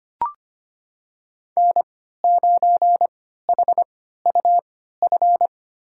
Morse code, heard as a keyed tone, about 25 wpm, spaced as for 14 wpm, 700 Hz. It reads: N9HUF
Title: USA Call Signs - 3x Character Spacing - 25wpm